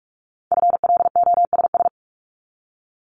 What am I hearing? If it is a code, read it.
Morse code, heard as a keyed tone, about 46 wpm, 700 Hz. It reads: FLOSS
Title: Multi-letter Phonemes 45wpm